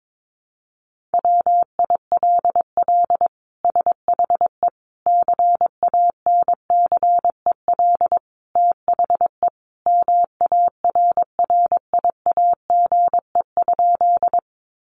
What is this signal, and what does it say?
Morse code, heard as a keyed tone, about 22 wpm, 700 Hz. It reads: WILL SHE CANCEL THE MARRIAGE?
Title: Sentences from Top 1200 Words 22wpm